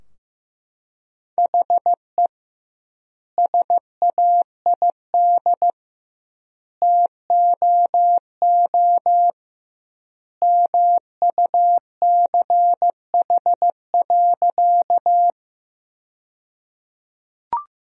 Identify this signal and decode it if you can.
Morse code, heard as a keyed tone, about 15 wpm, 700 Hz. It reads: HE SAID TOO MUCH.
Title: 2x Word-Spacing - Sentences from Top 200 Words - 15wpm